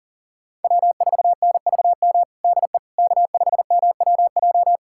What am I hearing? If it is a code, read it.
Morse code, heard as a keyed tone, about 37 wpm, 700 Hz. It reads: W4NVM DE K5MWJ
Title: USA Callsign DE Callsign - 5 Characters - 40wpm